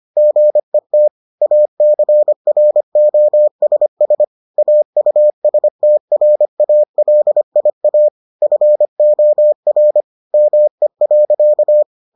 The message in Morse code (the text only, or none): GET ACROSS AUSTRALIA FOR ME.